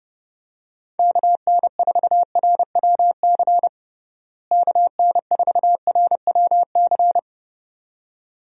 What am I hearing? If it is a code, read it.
Morse code, heard as a keyed tone, about 29 wpm, 700 Hz. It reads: KN4RWC KN4RWC
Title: USA Call Signs - 6 Characters - Repeated 2x - 30wpm